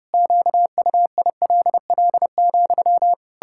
Morse code, heard as a keyed tone, about 30 wpm, 700 Hz. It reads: QUILL,
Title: Pangram Challenge v2 - 30wpm